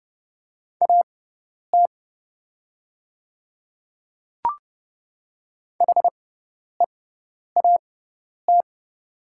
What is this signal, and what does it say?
Morse code, heard as a keyed tone, about 29 wpm, spaced as for 10 wpm, 690 Hz. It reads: AT HEAT
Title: 6x Character Spacing - Top 500 Words - 30wpm